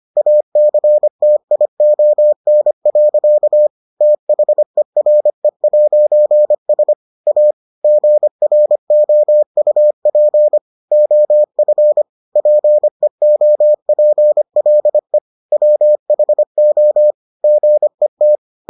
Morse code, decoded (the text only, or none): ACTION. THERE'S A GROUP OF PEOPLE WHO GET